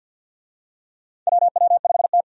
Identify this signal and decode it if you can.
Morse code, heard as a keyed tone, about 51 wpm, 700 Hz. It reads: WW5T
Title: USA Call Signs - 4 Characters - 50wpm